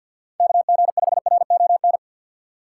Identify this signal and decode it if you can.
Morse code, heard as a keyed tone, about 53 wpm, 700 Hz. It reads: KG5RON
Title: USA Call Signs - 6 Characters - 3x Character Spacing - 50wpm